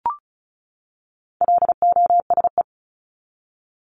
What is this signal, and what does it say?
Morse code, heard as a keyed tone, about 34 wpm, 700 Hz. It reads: LOSE
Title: Top 1001-1250 Words 35wpm